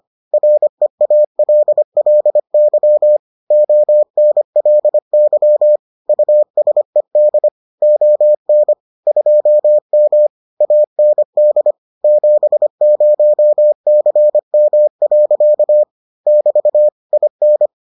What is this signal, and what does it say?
Morse code, heard as a keyed tone, about 25 wpm, 600 Hz. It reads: REALLY ONLY USED ON 2M AND 70CM. <BT> IN